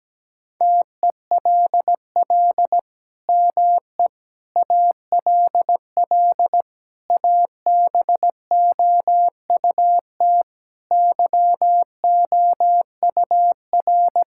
Morse code, heard as a keyed tone, about 17 wpm, 700 Hz. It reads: TELL ME ALL ABOUT YOUR
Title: Sentences from Top 1000 Words - Review Part 6 - 17wpm